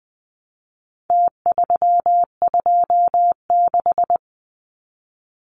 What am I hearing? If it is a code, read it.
Morse code, heard as a keyed tone, about 20 wpm, 700 Hz. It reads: T326
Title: Numbers - Contest Style - 4-Digit 20wpm